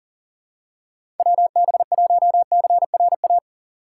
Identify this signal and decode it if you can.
Morse code, heard as a keyed tone, about 40 wpm, 700 Hz. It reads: WB1CRA
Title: USA Call Signs - 6 Characters - 4x Character Spacing - 40wpm